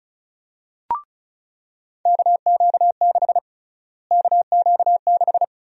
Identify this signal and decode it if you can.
Morse code, heard as a keyed tone, about 34 wpm, 700 Hz. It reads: KQ6 KQ6
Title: USA Call Sign Prefixes - Repeated 2x - 35wpm